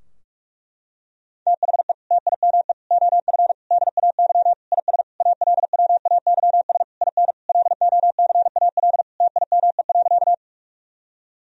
Morse code, heard as a keyed tone, about 41 wpm, 700 Hz. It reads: THE TIME OF DAY IS ALWAYS IN LOCAL TIME.
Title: Sentences from Top 1000 Words - Review Part 3 - 45wpm